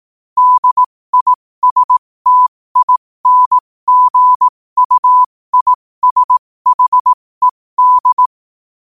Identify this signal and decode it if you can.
Morse code, heard as a keyed tone, about 18 wpm, spaced as for 15 wpm, 1000 Hz. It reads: DISTINGUISHED
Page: New quotes every day in morse code at 15 Words per minute.